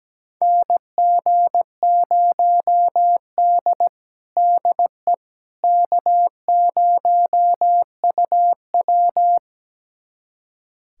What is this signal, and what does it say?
Morse code, heard as a keyed tone, about 17 wpm, 700 Hz. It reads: NG0D DE K0UW
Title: USA Callsign DE Callsign - 4 Characters - 17wpm